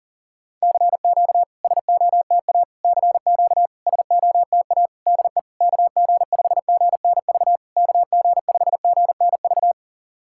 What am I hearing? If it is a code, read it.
Morse code, heard as a keyed tone, about 40 wpm, 700 Hz. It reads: CQ SOTA CQ SOTA DE KG5GNV KG5GNV
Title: SOTA QSO Practice 40wpm